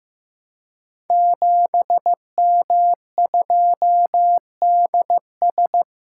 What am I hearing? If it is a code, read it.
Morse code, heard as a keyed tone, about 15 wpm, 700 Hz. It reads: 7M2DS
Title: Code Groups - 5 Letter:Numbers - 15wpm